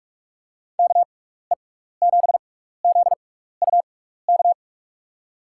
Morse code, heard as a keyed tone, about 46 wpm, spaced as for 16 wpm, 700 Hz. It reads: KE7ZUK
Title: USA Call Signs - 6x Character Spacing - 45wpm